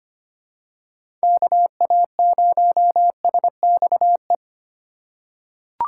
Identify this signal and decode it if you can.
Morse code, heard as a keyed tone, about 25 wpm, 700 Hz. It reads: KA0SXE
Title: USA Call Signs - 6 Characters - 4x Character Spacing - 25wpm